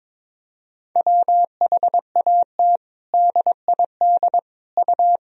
Morse code, heard as a keyed tone, about 22 wpm, 700 Hz. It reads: WHAT DID U
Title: Sentences from Top 100 Words and Abbreviations 22wpm